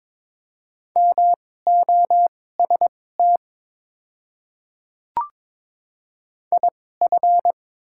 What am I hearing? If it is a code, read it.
Morse code, heard as a keyed tone, about 22 wpm, spaced as for 16 wpm, 700 Hz. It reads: MOST IF
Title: 2x Character Spacing - Top 300 Words - 22wpm